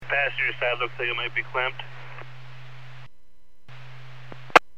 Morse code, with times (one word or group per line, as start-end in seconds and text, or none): none